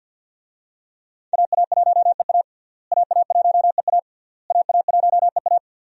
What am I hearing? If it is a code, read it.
Morse code, heard as a keyed tone, about 51 wpm, 700 Hz. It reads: AA1EA AA1EA AA1EA
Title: USA Call Signs - 5 Characters - Repeated 3x - 50wpm